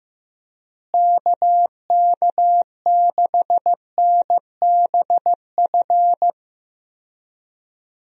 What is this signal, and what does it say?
Morse code, heard as a keyed tone, about 15 wpm, 700 Hz. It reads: KK6NBF
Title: USA Call Signs - 6 Characters - Repeated 2x - 15wpm